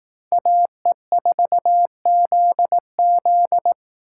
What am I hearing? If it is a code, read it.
Morse code, heard as a keyed tone, about 18 wpm, 700 Hz. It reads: AE4ZZ
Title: Speed Racing - USA 2x2 Call Signs 15wpm